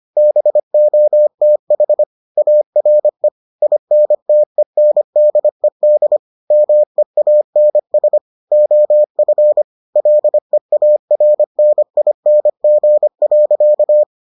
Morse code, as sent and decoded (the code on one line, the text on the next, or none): -... --- - .... / .- .-. / .. -. - . -. -.. -.. / -- . .- -. ... / --- ..-. / .-.. .- .-. -. .. -. --. .-.-.-
BOTH ARE INTENDED MEANS OF LEARNING.